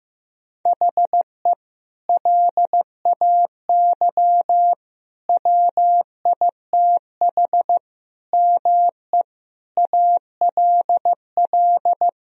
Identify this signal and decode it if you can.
Morse code, heard as a keyed tone, about 15 wpm, 700 Hz. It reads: HE LAY WITH ME ALL
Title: Sentences from Top 1000 Words - Review Part 2 - 15wpm